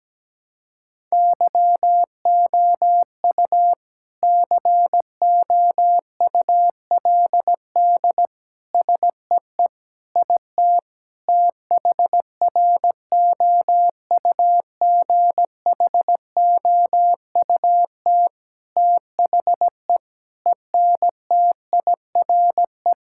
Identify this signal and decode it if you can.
Morse code, heard as a keyed tone, about 17 wpm, 700 Hz. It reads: YOU COULD SEE IT THROUGHOUT THE ENTIRE
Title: Sentences from Top 1000 Words - Review Part 5 - 17wpm